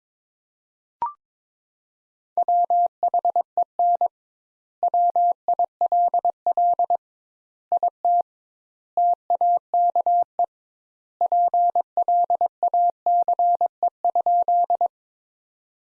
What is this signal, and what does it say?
Morse code, heard as a keyed tone, about 22 wpm, 700 Hz. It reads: WHEN WILL IT TAKE PLACE?
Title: Sentences Top 200 Words - 22wpm